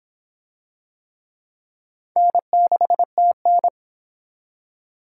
Morse code, heard as a keyed tone, about 25 wpm, 700 Hz. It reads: N6TN